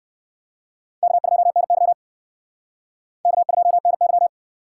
Morse code, heard as a keyed tone, about 57 wpm, 700 Hz. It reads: K2NY K2NY